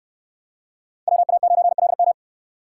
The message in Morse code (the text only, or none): WA9LM